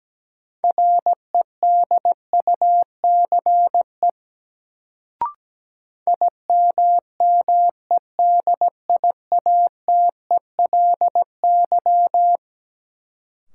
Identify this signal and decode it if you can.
Morse code, heard as a keyed tone, about 17 wpm, 700 Hz. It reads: REDUCE IMMEDIATELY
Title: Sentences from Top 1100 Words - New Word Drill - 17wpm